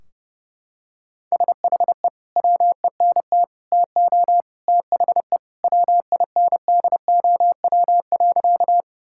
Morse code, heard as a keyed tone, about 30 wpm, 700 Hz. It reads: SHE WENT TO THE WINDOW.
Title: Sentences from Top 1600 Words 30wpm